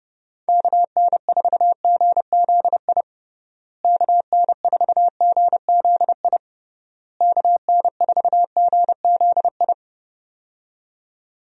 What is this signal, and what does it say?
Morse code, heard as a keyed tone, about 32 wpm, 700 Hz. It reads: KN4GZI KN4GZI KN4GZI
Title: USA Call Signs - 6 Characters - Repeated 3x - 30wpm